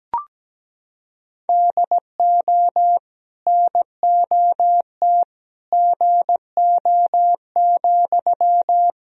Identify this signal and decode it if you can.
Morse code, heard as a keyed tone, about 17 wpm, 700 Hz. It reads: DO NOT GO,
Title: Sentences from Top 1000 Words - Review Part 1 - 17wpm